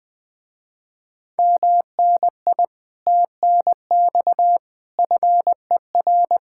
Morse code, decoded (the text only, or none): MNI TNX FER